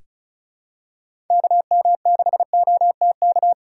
Morse code, heard as a keyed tone, about 36 wpm, 700 Hz. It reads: KM6OTK
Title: USA Call Signs - 6 Characters - 3x Character Spacing - 35wpm